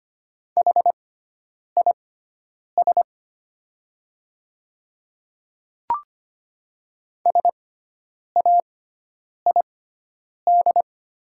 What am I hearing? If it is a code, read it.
Morse code, heard as a keyed tone, about 25 wpm, spaced as for 9 wpm, 690 Hz. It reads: HIS SAID
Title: 6x Character Spacing - Top 200 Words - 25wpm